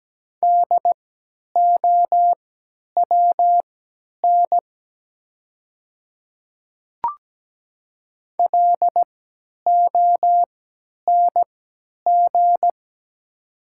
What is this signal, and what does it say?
Morse code, heard as a keyed tone, about 17 wpm, spaced as for 10 wpm, 700 Hz. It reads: DOWN LONG
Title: 3x Character Spacing - Top 100 Words - 17wpm